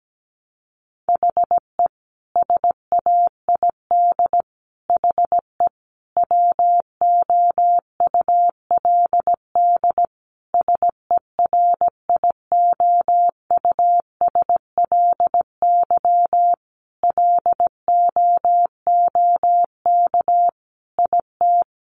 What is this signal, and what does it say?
Morse code, heard as a keyed tone, about 17 wpm, 700 Hz. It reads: HE SAID HE WOULD SERIOUSLY LOOK IT
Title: Sentences from Top 1200 Words 17wpm